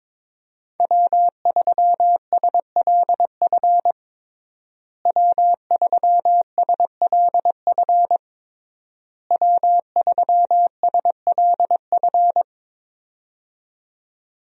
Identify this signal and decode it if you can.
Morse code, heard as a keyed tone, about 22 wpm, 700 Hz. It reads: W3SLF W3SLF W3SLF
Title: USA Call Signs - 5 Characters - Repeated 3x - 22wpm